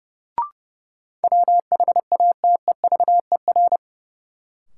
Morse code, heard as a keyed tone, about 30 wpm, 700 Hz. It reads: WHATEVER
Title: Sentences from Top 1100 Words - New Word Drill - 30wpm